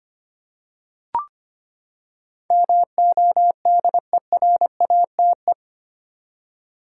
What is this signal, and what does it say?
Morse code, heard as a keyed tone, about 25 wpm, 700 Hz. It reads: MODERATE
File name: Eight Letter Words - 25wpm